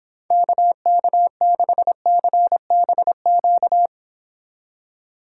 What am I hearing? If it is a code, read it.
Morse code, heard as a keyed tone, about 29 wpm, 700 Hz. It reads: KK6CBQ